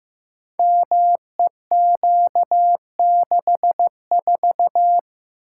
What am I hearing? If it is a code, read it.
Morse code, heard as a keyed tone, about 15 wpm, 700 Hz. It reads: MEQ64
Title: Code Groups - 5 Letter:Numbers - 15wpm